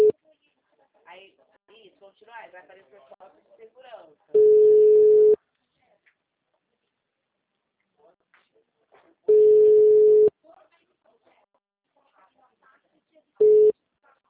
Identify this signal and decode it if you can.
Ringback tone (Europe, 425 Hz), with speech.